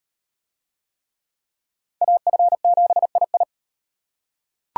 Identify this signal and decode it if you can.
Morse code, heard as a keyed tone, about 38 wpm, 700 Hz. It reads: AF7II